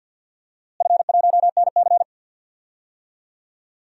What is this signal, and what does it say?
Morse code, heard as a keyed tone, about 52 wpm, 700 Hz. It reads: F1NC
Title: DXCC-100 Call Signs - 4 Characters - Repeated 2x - 50wpm